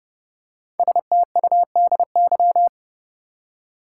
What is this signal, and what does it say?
Morse code, heard as a keyed tone, about 30 wpm, 700 Hz. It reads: STUDY
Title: Five Letter Words - 30wpm